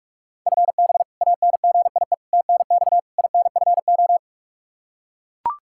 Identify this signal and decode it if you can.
Morse code, heard as a keyed tone, about 41 wpm, 700 Hz. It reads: FB ANGIE TNX INFO
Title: K1USN SST QSO Practice 45wpm